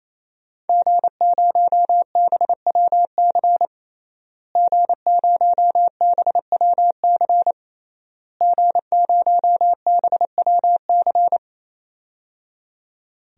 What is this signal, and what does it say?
Morse code, heard as a keyed tone, about 28 wpm, 700 Hz. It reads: G0BWC G0BWC G0BWC
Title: DXCC-100 Call Signs - 5 Characters - Repeated 3x - 28wpm